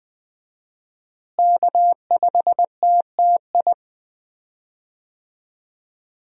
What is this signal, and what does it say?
Morse code, heard as a keyed tone, about 20 wpm, 700 Hz. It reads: K5TTI
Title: USA Call Signs - 5 Characters - Repeated 3x - 20wpm